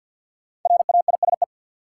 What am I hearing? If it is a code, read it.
Morse code, heard as a keyed tone, about 52 wpm, 700 Hz. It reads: RAISE
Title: Sentences from Top 1400 Words - New Word Drill - 50wpm